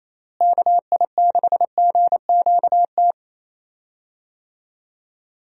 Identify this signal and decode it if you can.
Morse code, heard as a keyed tone, about 28 wpm, 700 Hz. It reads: KI6GQT
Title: USA Call Signs - 6 Characters - Repeated 2x - 28wpm